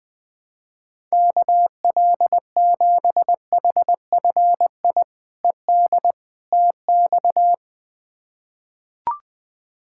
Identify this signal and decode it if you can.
Morse code, heard as a keyed tone, about 21 wpm, 700 Hz. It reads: KL7HFI ED TX
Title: K1USN SST QSO Practice - 20wpm